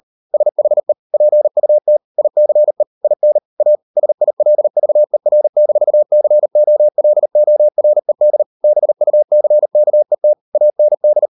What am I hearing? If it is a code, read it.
Morse code, heard as a keyed tone, about 38 wpm, 600 Hz. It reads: SHE PUT ICE IN A SILVER-COLORED BUCKET AND